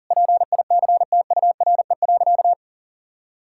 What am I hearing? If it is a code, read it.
Morse code, heard as a keyed tone, about 40 wpm, 700 Hz. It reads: PICTURE.
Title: Sentences from Top 1600 Words 40wpm